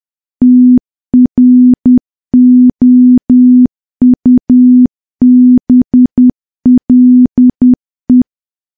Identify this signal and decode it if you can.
Morse code, heard as a keyed tone, about 10 wpm, 250 Hz. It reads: TROUBLE
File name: morse / morse-training-10WPM